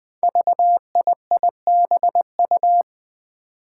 Morse code, heard as a keyed tone, about 20 wpm, 700 Hz. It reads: VIIBU
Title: Five Challenging Letter-Number Combinations 20wpm